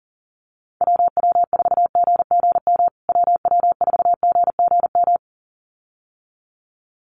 Morse code, heard as a keyed tone, about 40 wpm, 700 Hz. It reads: WW4GGM WW4GGM
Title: USA Call Signs Repeated 2x 40wpm